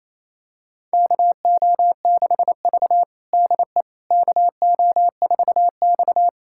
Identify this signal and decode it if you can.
Morse code, heard as a keyed tone, about 28 wpm, 700 Hz. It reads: KO6V DE KO4X
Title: USA Callsign DE Callsign - 4 Characters - 28wpm